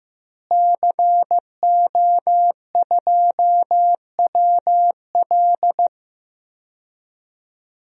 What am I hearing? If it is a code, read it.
Morse code, heard as a keyed tone, about 15 wpm, 700 Hz. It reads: CO2WL
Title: DXCC-100 Call Signs - 5 Characters - Repeated 3x - 15wpm